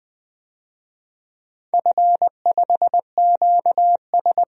Morse code, heard as a keyed tone, about 20 wpm, 700 Hz. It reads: F5QS
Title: DXCC-100 Call Signs - 4 Characters - 20wpm